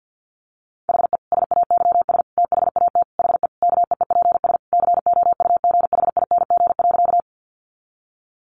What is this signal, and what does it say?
Morse code, heard as a keyed tone, about 50 wpm, 700 Hz. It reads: HE SAYS THAT HE KEEPS COUGHING.
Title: Additional Easy Phrases 50wpm